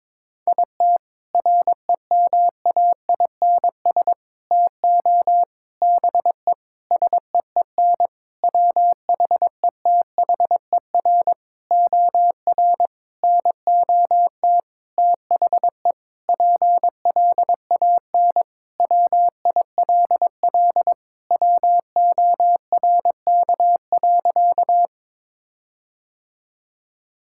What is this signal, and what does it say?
Morse code, heard as a keyed tone, about 22 wpm, 700 Hz. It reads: IT REMAINS TO BE SEEN WHETHER OR NOT THE PLAN WILL WORK.
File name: practice / 22wpm / Sentences with 8-Word Phrases - 22wpm